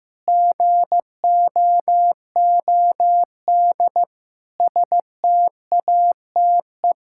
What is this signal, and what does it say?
Morse code, heard as a keyed tone, about 15 wpm, 700 Hz. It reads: GOOD STATE
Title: Sentences from Top 1000 Words - Review Part 5 - 15wpm